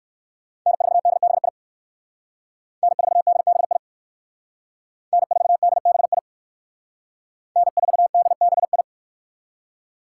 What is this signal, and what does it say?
Morse code, heard as a keyed tone, about 58 wpm, 700 Hz. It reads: N4DBI N4DBI N4DBI N4DBI